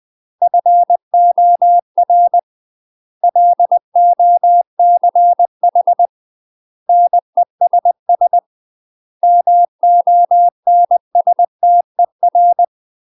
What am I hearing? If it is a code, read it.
Morse code, heard as a keyed tone, about 20 wpm, 700 Hz. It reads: FOR LOCH NESS MONSTER
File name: UK MASTER QSOs TRAINING / 20WPM / UK MASTER QSOs 0049